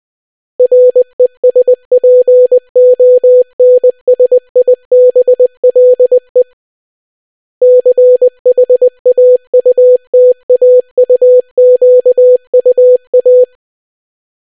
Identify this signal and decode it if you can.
Morse code, heard as a keyed tone, about 20 wpm, 500 Hz. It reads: RESPONSIBLE CHAUTAUQUA